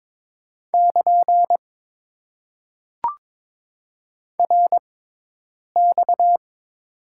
Morse code, heard as a keyed tone, about 22 wpm, spaced as for 8 wpm, 700 Hz. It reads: <KN> RX
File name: Top 90 QSO Elements - 6x Character Spacing - 22wpm